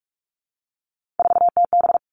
Morse code, heard as a keyed tone, about 44 wpm, 700 Hz. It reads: 4TB
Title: All Cut Numbers - 3-Digit 45wpm